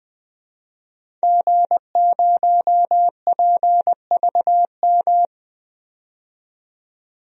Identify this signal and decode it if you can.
Morse code, heard as a keyed tone, about 20 wpm, 700 Hz. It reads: G0PVM
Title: DXCC-100 Call Signs - 5 Characters - 20wpm